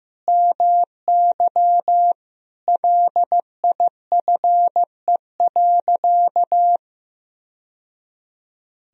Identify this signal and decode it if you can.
Morse code, heard as a keyed tone, about 15 wpm, 700 Hz. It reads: MY LIFE.
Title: Sentences from Top 1000 Words - Review Part 1 - 15wpm